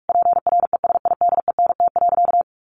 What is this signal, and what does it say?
Morse code, heard as a keyed tone, about 44 wpm, 700 Hz. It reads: PRESIDENT.
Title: Sentences from Top 1000 Words 45wpm